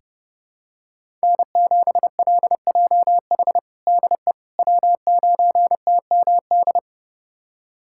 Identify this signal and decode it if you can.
Morse code, heard as a keyed tone, about 32 wpm, 700 Hz. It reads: N7LJH DE W9TMD
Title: USA Callsign DE Callsign - 5 Characters - 30wpm